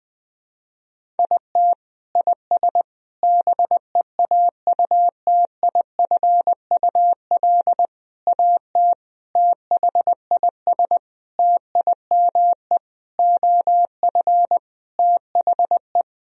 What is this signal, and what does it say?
Morse code, heard as a keyed tone, about 20 wpm, 700 Hz. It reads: IT IS BEAUTIFUL AT THIS TIME OF THE
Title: Sentences from Top 1000 Words - Review Part 1 - 20wpm